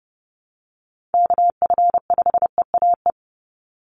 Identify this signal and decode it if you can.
Morse code, heard as a keyed tone, about 30 wpm, 700 Hz. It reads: KF5EAE
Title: USA Call Signs 30wpm